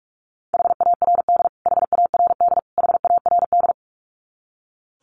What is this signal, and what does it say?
Morse code, heard as a keyed tone, about 44 wpm, 700 Hz. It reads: HARD HARD HARD
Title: Top 500 Words Repeated 45wpm